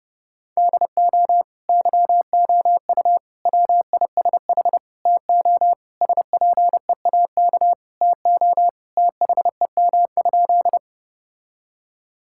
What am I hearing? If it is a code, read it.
Morse code, heard as a keyed tone, about 30 wpm, 700 Hz. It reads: DO YOU WISH TO SPEAK TO THEM?
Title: Sentences from Top 1600 Words 30wpm